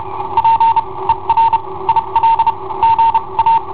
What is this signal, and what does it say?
Morse code, heard as a keyed tone, about 32 wpm, spaced as for 26 wpm, 910 Hz. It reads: P ER IL GA
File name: i7ohp / cassetta i7ohp B